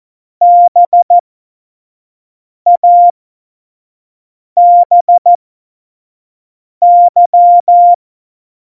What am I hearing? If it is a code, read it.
Morse code, heard as a keyed tone, about 14 wpm, spaced as for 5 wpm, 700 Hz. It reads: BABY
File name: Morsekurs / Witze / Witz 04. September 2018, 25 F70